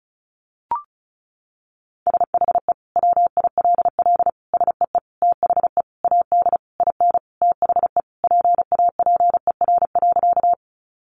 Morse code, heard as a keyed tone, about 34 wpm, 700 Hz. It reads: SHE WILL SEE THE AD IN THE PAPER.
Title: Sentences from Top 1300 Words 35wpm